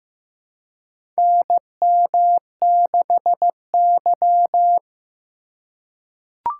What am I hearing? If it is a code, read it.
Morse code, heard as a keyed tone, about 15 wpm, 700 Hz. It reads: NM6Y
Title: USA Call Signs - 4 Characters - 4x Character Spacing - 15wpm